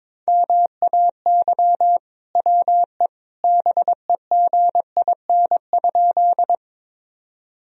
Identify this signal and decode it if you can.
Morse code, heard as a keyed tone, about 22 wpm, 700 Hz. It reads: MAY WE BEGIN?
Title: Sentences from Top 1000 Words - Review Part 1 - 22wpm